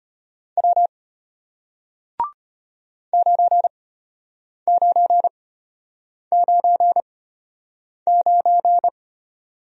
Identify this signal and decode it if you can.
Morse code, heard as a keyed tone, about 34 wpm, spaced as for 8 wpm, 700 Hz. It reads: W 9999